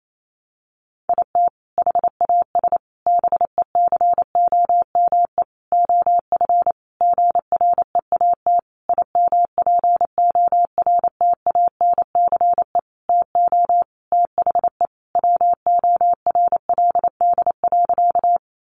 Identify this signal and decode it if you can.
Morse code, heard as a keyed tone, about 28 wpm, 700 Hz. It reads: IT HAS BECOME OF GREAT IMPORTANCE TO THE WORLD.
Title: Sentences from Top 1200 Words 28wpm